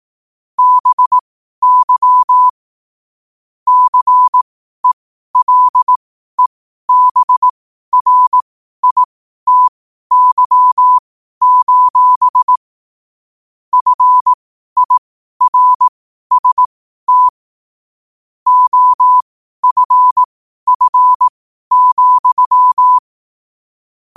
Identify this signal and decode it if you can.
Morse code, heard as a keyed tone, about 18 wpm, spaced as for 12 wpm, 1000 Hz. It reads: BY CELEBRITY: FIRST OFF,
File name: qotdfiles / 12wpm